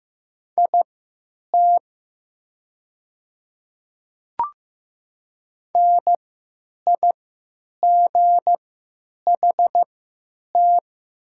Morse code, heard as a keyed tone, about 15 wpm, spaced as for 9 wpm, 690 Hz. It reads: IT NIGHT
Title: 3x Character Spacing - Top 300 Words - 15wpm